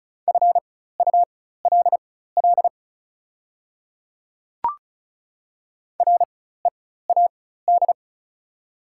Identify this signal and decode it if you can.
Morse code, heard as a keyed tone, about 36 wpm, spaced as for 16 wpm, 690 Hz. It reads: FULL READ
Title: 4x Character Spacing - Top 500 Words - 35wpm